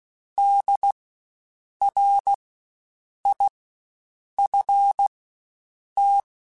Morse code, heard as a keyed tone, about 16 wpm, spaced as for 7 wpm, 800 Hz. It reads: DRIFT
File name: Spaced Alphas / Spaced Alphas mp3s / Ds sp @15